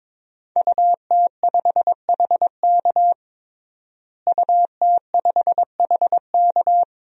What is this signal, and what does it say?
Morse code, heard as a keyed tone, about 22 wpm, 700 Hz. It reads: UT5HK UT5HK
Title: DXCC-100 Call Signs - 5 Characters - Repeated 2x - 22wpm